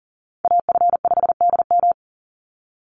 Morse code, heard as a keyed tone, about 40 wpm, 700 Hz. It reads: AF5DM
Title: Call Signs Repeated 2x - 40wpm